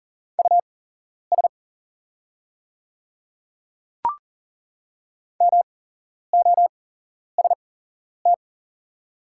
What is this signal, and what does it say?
Morse code, heard as a keyed tone, about 40 wpm, spaced as for 11 wpm, 700 Hz. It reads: US MOST
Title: 8x Character Spacing - Top 200 Words - 40wpm